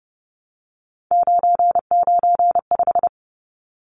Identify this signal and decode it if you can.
Morse code, heard as a keyed tone, about 30 wpm, 700 Hz. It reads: 995
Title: Numbers - 3-Digit 30wpm